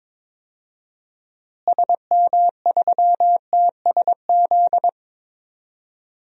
Morse code, heard as a keyed tone, about 22 wpm, 700 Hz. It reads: SM3TSZ
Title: DXCC-100 Call Signs - 6 Characters - Repeated 2x - 22wpm